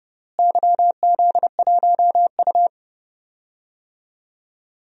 Morse code, heard as a keyed tone, about 30 wpm, 700 Hz. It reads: YZ1U
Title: DXCC-100 Call Signs - 4 Characters - Repeated 3x - 30wpm